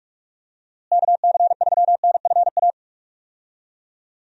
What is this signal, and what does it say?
Morse code, heard as a keyed tone, about 43 wpm, 700 Hz. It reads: KC3NFA
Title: USA Call Signs - 6 Characters - Repeated 2x - 45wpm